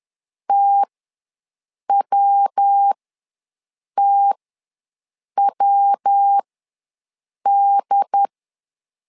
Morse code, heard as a keyed tone, about 10 wpm, spaced as for 6 wpm, 800 Hz. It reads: TWTWD
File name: LEZIONI / 3 MWRO / 3 MW ws 10wpm 2213